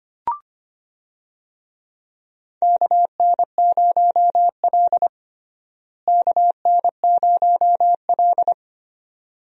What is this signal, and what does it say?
Morse code, heard as a keyed tone, about 25 wpm, 700 Hz. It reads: KN0L KN0L
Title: USA Call Signs - 4 Characters - Repeated 2x - 25wpm